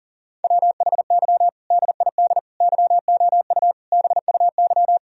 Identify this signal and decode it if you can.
Morse code, heard as a keyed tone, about 40 wpm, 700 Hz. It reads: WHY DID YOU BUY
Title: Sentences from Top 1000 Words - Review Part 4 - 40wpm